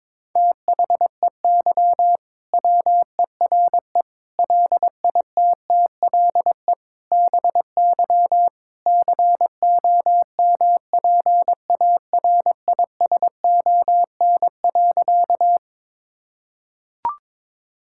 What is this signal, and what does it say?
Morse code, heard as a keyed tone, about 22 wpm, 700 Hz. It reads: THEY WERE LITTLE BY COMPARISON.
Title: Sentences from Top 1600 Words 22wpm